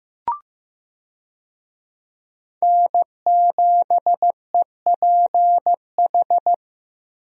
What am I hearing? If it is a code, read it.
Morse code, heard as a keyed tone, about 15 wpm, 700 Hz. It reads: N7EPH
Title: USA Call Signs - 5 Characters - Repeated 2x - 15wpm